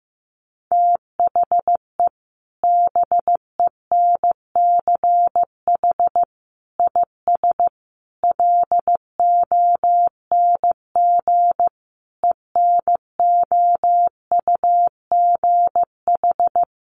Morse code, heard as a keyed tone, about 15 wpm, 700 Hz. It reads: THE BENCH IS LONG ENOUGH
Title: Sentences from Top 1400 Words 15wpm